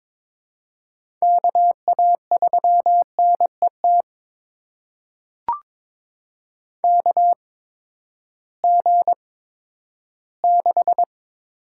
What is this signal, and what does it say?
Morse code, heard as a keyed tone, about 22 wpm, spaced as for 13 wpm, 700 Hz. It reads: KA3NET KG6
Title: USA Call Signs - 6 Characters - 8x Character Spacing - 22wpm